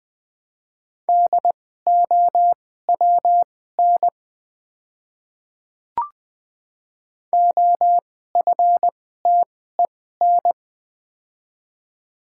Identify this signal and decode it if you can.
Morse code, heard as a keyed tone, about 20 wpm, spaced as for 14 wpm, 700 Hz. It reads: DOWN OFTEN
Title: 2x Character Spacing - Top 300 Words - 20wpm